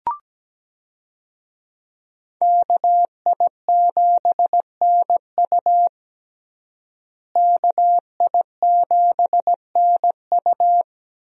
Morse code, heard as a keyed tone, about 17 wpm, 700 Hz. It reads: KI7NU KI7NU
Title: USA Call Signs - 5 Characters - Repeated 3x - 17wpm